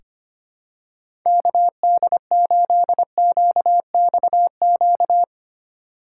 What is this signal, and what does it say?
Morse code, heard as a keyed tone, about 25 wpm, 700 Hz. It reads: KD8QXQ
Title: USA Call Signs - 4x Character Spacing - 25wpm